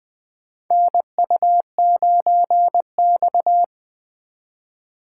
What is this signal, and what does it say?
Morse code, heard as a keyed tone, about 20 wpm, 700 Hz. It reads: NU9X
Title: USA Call Signs - 4 Characters - 4x Character Spacing - 20wpm